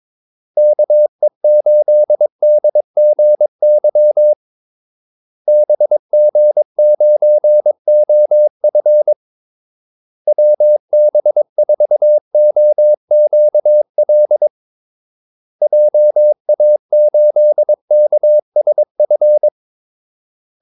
Morse code, heard as a keyed tone, about 22 wpm, 600 Hz. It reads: KE8DGY BG9OF WB4OQL JA8KSF
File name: Callsigns-22wpm